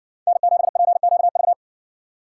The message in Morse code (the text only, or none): N7PQV